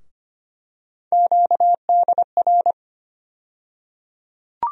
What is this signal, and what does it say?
Morse code, heard as a keyed tone, about 25 wpm, 700 Hz. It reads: QDR
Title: USA Call Sign Postfixes - 3 Letters 25wpm